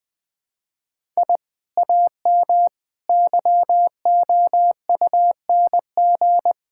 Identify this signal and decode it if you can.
Morse code, heard as a keyed tone, about 20 wpm, 700 Hz. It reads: I AM YOUNG
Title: Sentences from Top 1000 Words - Review Part 5 - 20wpm